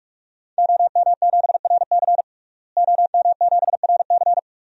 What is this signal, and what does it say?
Morse code, heard as a keyed tone, about 47 wpm, 700 Hz. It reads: OM7RC OM7RC
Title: DXCC-100 Call Signs - 5 Characters - Repeated 2x - 45wpm